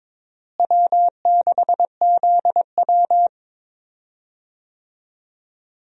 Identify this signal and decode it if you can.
Morse code, heard as a keyed tone, about 22 wpm, 700 Hz. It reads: W6ZW
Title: USA Call Signs - 4 Characters - Repeated 2x - 22wpm